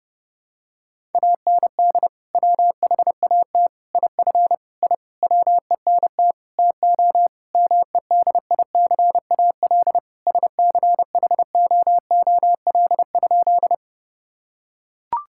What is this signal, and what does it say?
Morse code, heard as a keyed tone, about 32 wpm, 700 Hz. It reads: AND WHAT IF I WENT TO MEDICAL SCHOOL?
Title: Sentences from Top 1000 Words - Review Part 5 - 30wpm